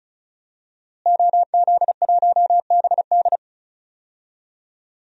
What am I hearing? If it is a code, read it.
Morse code, heard as a keyed tone, about 36 wpm, 700 Hz. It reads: OZ1BD
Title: DXCC-100 Call Signs - 5 Characters - Repeated 2x - 35wpm